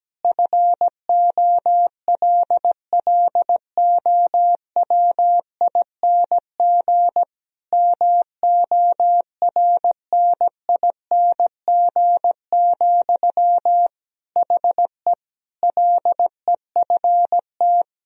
Morse code, heard as a keyed tone, about 17 wpm, 700 Hz. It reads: FOLLOWING MORNING, HE LEFT
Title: Sentences from Top 1000 Words - Review Part 4 - 17wpm